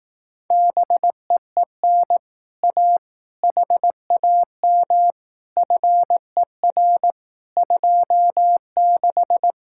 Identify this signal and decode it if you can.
Morse code, heard as a keyed tone, about 18 wpm, 700 Hz. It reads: BEEN A HAM FER 26
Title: Speed-Racing Common QSO Phrases Encore - 15wpm